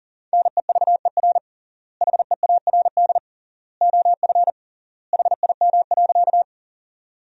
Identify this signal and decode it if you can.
Morse code, heard as a keyed tone, about 40 wpm, 700 Hz. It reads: NEVER HEARD OF HIM.
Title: 3x Word-Spacing - Sentences from Top 400 Words - 40wpm